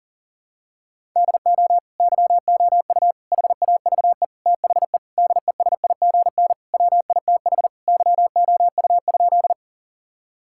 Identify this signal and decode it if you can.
Morse code, heard as a keyed tone, about 40 wpm, 700 Hz. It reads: DO YOU HAVE THE DESIGN WITH YOU?
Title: Sentences from Top 1000 Words - Review Part 5 - 40wpm